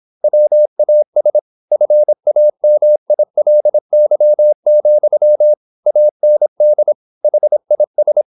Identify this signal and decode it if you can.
Morse code, heard as a keyed tone, about 26 wpm, 600 Hz. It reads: WAS FAMILY, AND HIS
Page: Never 26wpm | CW med Gnister